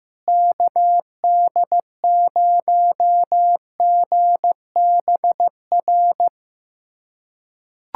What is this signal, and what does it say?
Morse code, heard as a keyed tone, about 15 wpm, 700 Hz. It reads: KD0GBR
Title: USA Callsign DE Callsign 15wpm